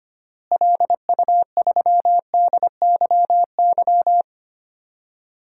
Morse code, heard as a keyed tone, about 25 wpm, 700 Hz. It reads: LU3DYY
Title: DXCC-100 Call Signs - 6 Characters - Repeated 2x - 25wpm